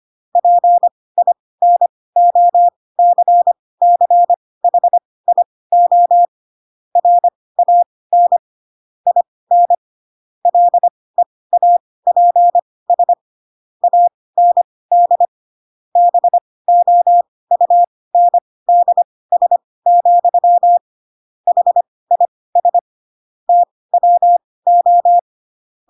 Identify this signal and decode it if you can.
Morse code, heard as a keyed tone, about 25 wpm, spaced as for 18 wpm, 700 Hz. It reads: PINOCCHIO RAN IN LEAPS AND BOUNDS, HIS TWO
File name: Pinocchio ch3 12 25 ik1hge